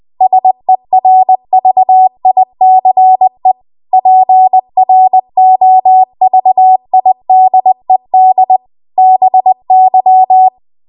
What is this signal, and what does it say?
Morse code, heard as a keyed tone, about 20 wpm, 750 Hz. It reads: SERVICE PROVIDED BY
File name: Morse / 20WPM